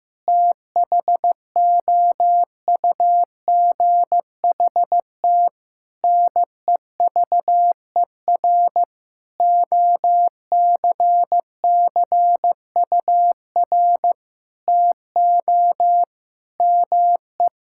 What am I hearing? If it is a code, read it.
Morse code, heard as a keyed tone, about 15 wpm, 700 Hz. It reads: THOUGHT NEVER OCCUR TO ME
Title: Sentences from Top 1000 Words - Review Part 6 - 15wpm